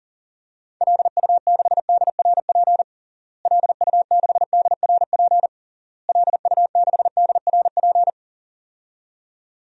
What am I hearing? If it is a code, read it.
Morse code, heard as a keyed tone, about 40 wpm, 700 Hz. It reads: LU6DRP LU6DRP LU6DRP
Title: DXCC-100 Call Signs - 6 Characters - Repeated 3x - 40wpm